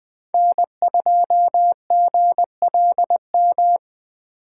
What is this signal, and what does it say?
Morse code, heard as a keyed tone, about 20 wpm, 700 Hz. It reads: N2GLM
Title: Speed Racing - USA 1x3 Call Signs 17wpm